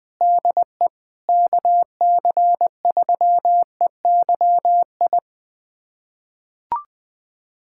Morse code, heard as a keyed tone, about 20 wpm, 700 Hz. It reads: DE KC3EYI
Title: USA Callsign DE Callsign - 6 Characters - 20wpm